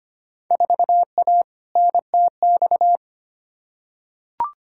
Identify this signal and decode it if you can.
Morse code, heard as a keyed tone, about 25 wpm, 700 Hz. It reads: 4A NTX
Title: Field Day Exchanges - 25wpm